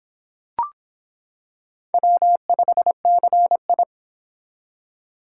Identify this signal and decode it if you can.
Morse code, heard as a keyed tone, about 26 wpm, 700 Hz. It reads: W5CI